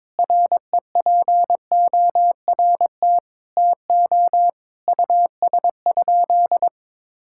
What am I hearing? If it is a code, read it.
Morse code, heard as a keyed tone, about 22 wpm, 700 Hz. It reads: REPORT TO US?
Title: Sentences from Top 1000 Words - Review Part 6 - 22wpm